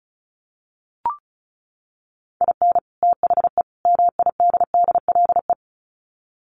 Morse code, heard as a keyed tone, about 34 wpm, 700 Hz. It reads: IN THE MIDDLE
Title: Sets of 3 Words 35wpm